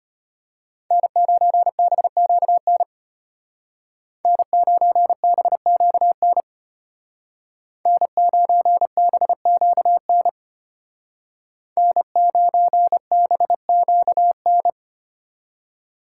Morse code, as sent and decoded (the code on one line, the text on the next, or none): -. ----. -... --.- -. / -. ----. -... --.- -. / -. ----. -... --.- -. / -. ----. -... --.- -.
N9BQN N9BQN N9BQN N9BQN